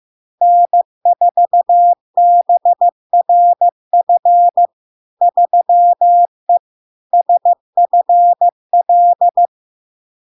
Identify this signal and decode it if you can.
Morse code, heard as a keyed tone, about 15 wpm, 700 Hz. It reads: N4BRF 3E SFL
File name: LARK-FD-2022-Log-in-CW-15wpm-700-Hz